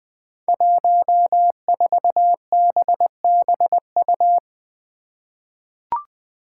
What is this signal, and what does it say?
Morse code, heard as a keyed tone, about 20 wpm, 700 Hz. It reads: 14BBU
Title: Five Challenging Letter-Number Combinations 20wpm